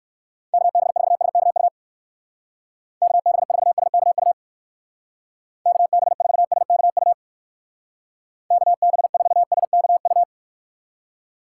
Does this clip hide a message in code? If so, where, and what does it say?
Yes: Morse code from 0.5 to 10.2 s, heard as a keyed tone, about 64 wpm, 700 Hz. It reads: KB4SKU KB4SKU KB4SKU KB4SKU